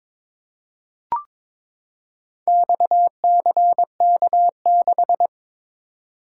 Morse code, heard as a keyed tone, about 22 wpm, 700 Hz. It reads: XCK6
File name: practice / 22wpm / Four Letter-Number Combinations - 22wpm